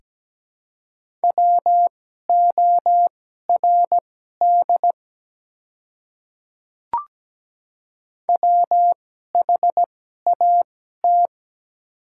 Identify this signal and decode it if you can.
Morse code, heard as a keyed tone, about 17 wpm, spaced as for 12 wpm, 700 Hz. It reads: WORD WHAT
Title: 2x Character Spacing - Top 300 Words - 17wpm